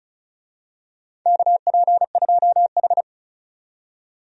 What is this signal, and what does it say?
Morse code, heard as a keyed tone, about 34 wpm, 700 Hz. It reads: KP2H
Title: DXCC-100 Call Signs - 4 Characters - 35wpm